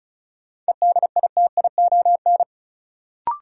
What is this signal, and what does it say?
Morse code, heard as a keyed tone, about 36 wpm, 700 Hz. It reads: EDITION
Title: Sets of 5 Words - New Word Drill - 35wpm